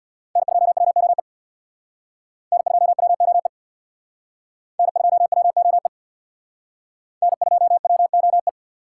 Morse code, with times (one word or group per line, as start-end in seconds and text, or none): 0.4-1.2 N2WOE
2.5-3.5 N2WOE
4.8-5.9 N2WOE
7.2-8.5 N2WOE